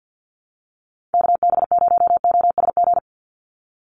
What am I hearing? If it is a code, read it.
Morse code, heard as a keyed tone, about 50 wpm, 700 Hz. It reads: KB0OSG
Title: USA Call Signs 50wpm